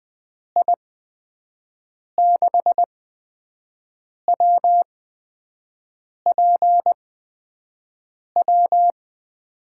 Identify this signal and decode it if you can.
Morse code, heard as a keyed tone, about 21 wpm, spaced as for 5 wpm, 700 Hz. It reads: I6WPW
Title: USA Call Signs - 6 Characters - 8x Character Spacing - 20wpm